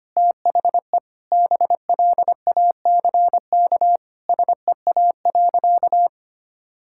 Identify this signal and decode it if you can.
Morse code, heard as a keyed tone, about 25 wpm, 700 Hz. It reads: THE BLACK SEA.
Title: Sentences from Top 1000 Words - Review Part 5 - 25wpm